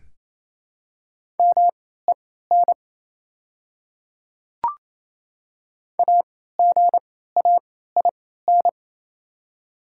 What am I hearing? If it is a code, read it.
Morse code, heard as a keyed tone, about 27 wpm, spaced as for 16 wpm, 700 Hz. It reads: MEN AGAIN
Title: 3x Character Spacing - Top 300 Words - 28wpm